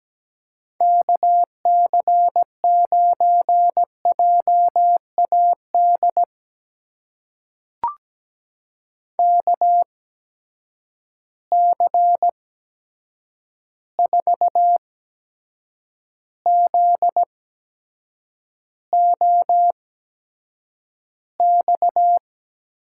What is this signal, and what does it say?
Morse code, heard as a keyed tone, about 17 wpm, spaced as for 7 wpm, 700 Hz. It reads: KC9JAD KC4ZOX
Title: USA Call Signs - 6 Characters - 8x Character Spacing - 17wpm